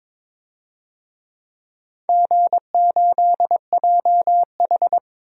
Morse code, heard as a keyed tone, about 23 wpm, 700 Hz. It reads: G8JH
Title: DXCC-100 Call Signs - 4 Characters - Repeated 3x - 22wpm